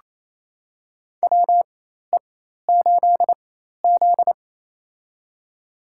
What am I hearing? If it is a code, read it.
Morse code, heard as a keyed tone, about 28 wpm, spaced as for 13 wpm, 700 Hz. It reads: WE8Z
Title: USA Call Signs - 4x Character Spacing - 28wpm